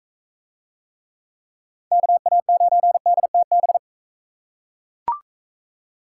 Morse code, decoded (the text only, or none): KA9DTB